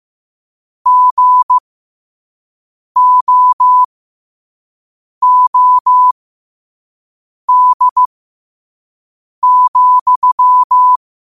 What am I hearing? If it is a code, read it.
Morse code, heard as a keyed tone, about 15 wpm, spaced as for 5 wpm, 1000 Hz. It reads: GOOD,
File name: qotdfiles / 5wpm